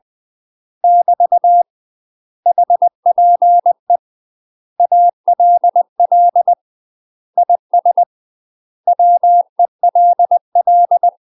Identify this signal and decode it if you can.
Morse code, heard as a keyed tone, about 20 wpm, 700 Hz. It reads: <BT> HPE ALL IS WELL
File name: UK MASTER QSOs TRAINING / 20WPM / UK MASTER QSOs 0007